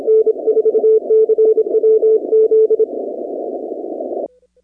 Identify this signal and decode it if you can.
Morse code, heard as a keyed tone, about 27 wpm, 440 Hz. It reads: N4CWZ
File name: 2019-cw-wpx